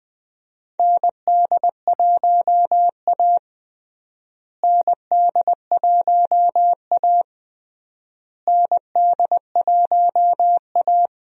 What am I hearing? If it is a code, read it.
Morse code, heard as a keyed tone, about 19 wpm, 700 Hz. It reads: ND1A ND1A ND1A
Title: USA Call Signs - 4 Characters - Repeated 3x - 20wpm